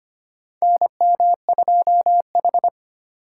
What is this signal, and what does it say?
Morse code, heard as a keyed tone, about 25 wpm, 700 Hz. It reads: NM2H
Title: USA Call Signs - 4 Characters - 4x Character Spacing - 25wpm